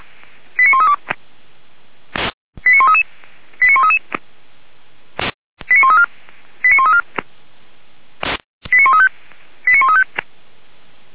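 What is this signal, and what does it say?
Five-tone selective call (ZVEI), heard as quick runs of five tones, tones of about 70 ms. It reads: 89132 89133 89133 89134 89134 89135 89135
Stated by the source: ZVEI five-tone selective call received via SDR.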